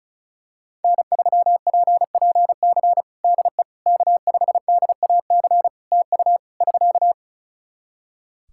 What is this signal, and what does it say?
Morse code, heard as a keyed tone, about 36 wpm, 700 Hz. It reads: N3PPC DE K5DAC TU <SK>
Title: K1USN SST QSO Practice 35wpm